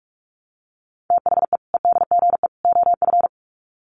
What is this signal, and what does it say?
Morse code, heard as a keyed tone, about 44 wpm, 700 Hz. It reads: THE EDGE OF
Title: Sets of 3 Words 45wpm